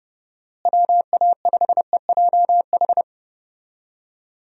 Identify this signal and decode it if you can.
Morse code, heard as a keyed tone, about 30 wpm, 700 Hz. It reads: WA5EJH
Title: USA Call Signs - 6 Characters - Repeated 2x - 30wpm